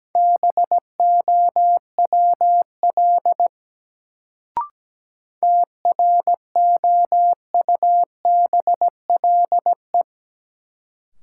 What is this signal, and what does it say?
Morse code, heard as a keyed tone, about 17 wpm, 700 Hz. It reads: BOWL TROUBLE
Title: Sentences from Top 1200 Words - New Word Drill Alt - 17wpm